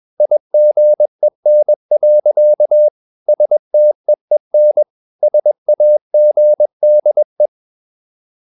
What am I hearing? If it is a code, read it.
Morse code, heard as a keyed tone, about 21 wpm, 600 Hz. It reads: IGEN. STEEN SAGDE
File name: Ferie QRS 21wpm